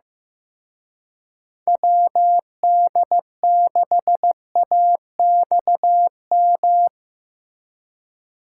Morse code, heard as a keyed tone, about 17 wpm, 700 Hz. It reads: WD6AXM
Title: Speed Racing - USA 2x3 Call Signs 15wpm